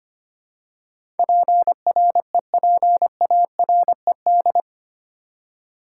Morse code, heard as a keyed tone, about 25 wpm, 700 Hz. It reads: PREPARED
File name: Eight Letter Words - 25wpm